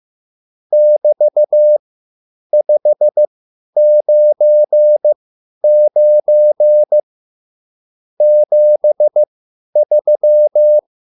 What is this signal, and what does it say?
Morse code, heard as a keyed tone, about 15 wpm, spaced as for 10 wpm, 600 Hz. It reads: <BT>599 73
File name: QSL / CW lesson